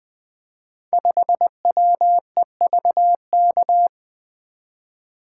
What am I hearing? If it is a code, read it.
Morse code, heard as a keyed tone, about 20 wpm, 700 Hz. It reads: 5WEVK
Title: Code Groups - 5 Letter:Numbers - 20wpm